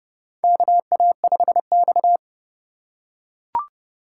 Morse code, heard as a keyed tone, about 31 wpm, 700 Hz. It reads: KA5X
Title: USA Call Signs - 4 Characters - 2x Character Spacing - 30wpm